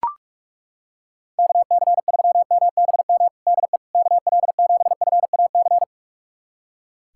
Morse code, heard as a keyed tone, about 42 wpm, 700 Hz. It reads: KC3MBM DE KL7FAC
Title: USA Callsign DE Callsign 45wpm